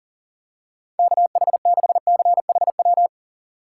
Morse code, heard as a keyed tone, about 40 wpm, 700 Hz. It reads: KH6CHW
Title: USA Call Signs - 6 Characters - 2x Character Spacing - 40wpm